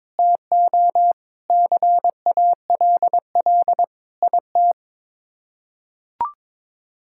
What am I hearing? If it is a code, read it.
Morse code, heard as a keyed tone, about 22 wpm, 700 Hz. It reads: TO CALL IT
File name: practice / 22wpm / Sets of 7 Words - 22wpm